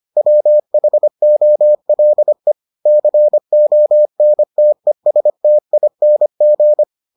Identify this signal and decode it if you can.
Morse code, heard as a keyed tone, about 25 wpm, 600 Hz. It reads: WHOLE CONTESTING